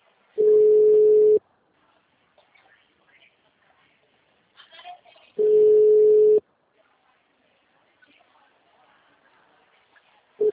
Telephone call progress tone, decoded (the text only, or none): ringback tone